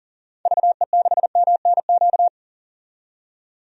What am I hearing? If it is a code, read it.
Morse code, heard as a keyed tone, about 40 wpm, 700 Hz. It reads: VE6MNQ
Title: DXCC-100 Call Signs - 6 Characters - 40wpm